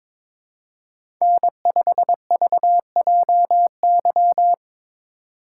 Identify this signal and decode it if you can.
Morse code, heard as a keyed tone, about 23 wpm, 700 Hz. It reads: N5VJY
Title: USA Call Signs - 3x Character Spacing - 22wpm